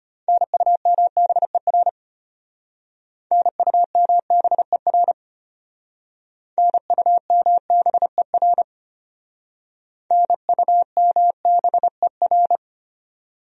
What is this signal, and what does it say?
Morse code, heard as a keyed tone, about 31 wpm, 700 Hz. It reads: NUMBER NUMBER NUMBER NUMBER
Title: Speed-Racing Top 100 Words 25wpm